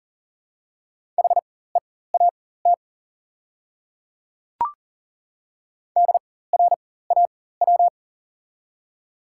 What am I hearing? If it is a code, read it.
Morse code, heard as a keyed tone, about 40 wpm, spaced as for 19 wpm, 700 Hz. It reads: HEAT DRAW
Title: 4x Character Spacing - Top 500 Words - 40wpm